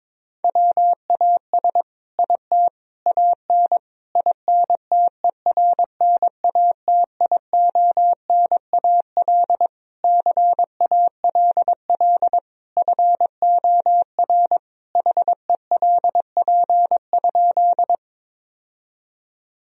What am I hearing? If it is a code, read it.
Morse code, heard as a keyed tone, about 22 wpm, 700 Hz. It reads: WAS IT AN INTERNATIONAL CALL FOR HELP?
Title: Sentences from Top 1000 Words - Review Part 4 - 22wpm